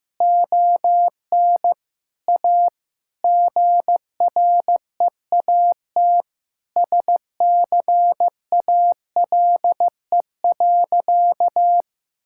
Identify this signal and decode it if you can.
Morse code, heard as a keyed tone, about 15 wpm, 700 Hz. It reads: ON A GREAT SCALE.
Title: Sentences from Top 1000 Words - Review Part 5 - 15wpm